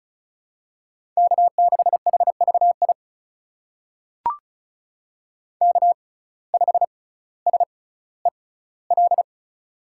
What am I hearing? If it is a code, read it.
Morse code, heard as a keyed tone, about 36 wpm, spaced as for 18 wpm, 700 Hz. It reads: K6HVI K5SEL
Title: USA Call Signs - 5 Characters - 6x Character Spacing - 35wpm